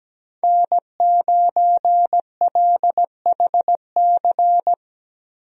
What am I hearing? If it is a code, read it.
Morse code, heard as a keyed tone, about 17 wpm, 700 Hz. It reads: N9LHC
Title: USA Call Signs - 5 Characters - 17wpm